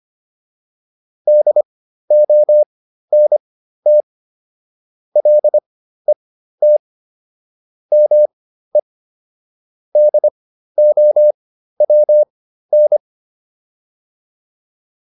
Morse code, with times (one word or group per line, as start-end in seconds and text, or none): 1.3-4.0 DONT
5.2-6.8 LET
7.9-8.8 ME
10.0-13.0 DOWN